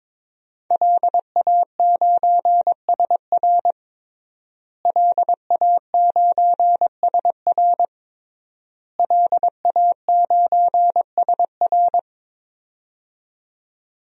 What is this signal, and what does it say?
Morse code, heard as a keyed tone, about 22 wpm, 700 Hz. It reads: LA9SR LA9SR LA9SR
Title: DXCC-100 Call Signs - 5 Characters - Repeated 3x - 22wpm